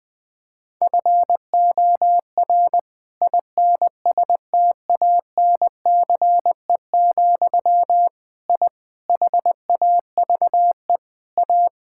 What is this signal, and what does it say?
Morse code, heard as a keyed tone, about 20 wpm, 700 Hz. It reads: FOR INSTANCE, I HAVE A
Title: Sentences from Top 1000 Words - Review Part 5 - 20wpm